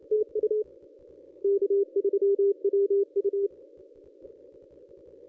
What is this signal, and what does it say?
Morse code, heard as a keyed tone, about 28 wpm, 440 Hz. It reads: TU K3WU
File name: contests / ss-cw-k5tr-op-n5rz -1100